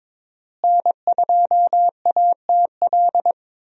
Morse code, heard as a keyed tone, about 22 wpm, 700 Hz. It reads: N2ATL
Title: USA Call Signs - 5 Characters - Repeated 3x - 22wpm